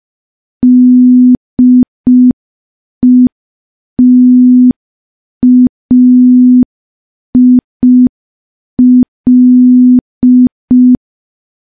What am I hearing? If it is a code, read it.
Morse code, heard as a keyed tone, about 5 wpm, 250 Hz. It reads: DETAIL
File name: morse / morse-training-05WPM